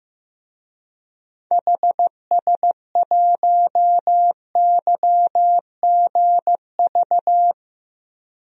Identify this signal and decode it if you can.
Morse code, heard as a keyed tone, about 16 wpm, 700 Hz. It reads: HS1YGV
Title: DXCC-100 Call Signs - 6 Characters - Repeated 2x - 15wpm